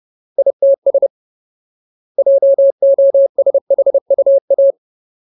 Morse code, heard as a keyed tone, about 30 wpm, 550 Hz. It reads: ITS JOSHUA
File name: Ultimate Word List / 30WPM Ultimate